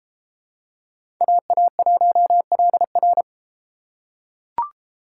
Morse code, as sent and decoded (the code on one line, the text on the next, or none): .- .- .---- .-.. .-.
AA1LR